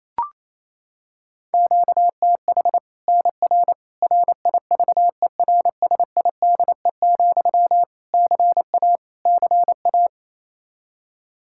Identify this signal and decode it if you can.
Morse code, heard as a keyed tone, about 28 wpm, 700 Hz. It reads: QTH NR RIVERSIDE, CA CA
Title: Practice QSO - Exchange 1 - Beginning - 28wpm